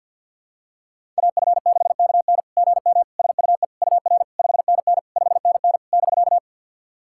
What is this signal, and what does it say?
Morse code, heard as a keyed tone, about 51 wpm, 690 Hz. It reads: AF6KN GM SUE UR 5NN 5NN <BK>
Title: SOTA QSO Practice 50wpm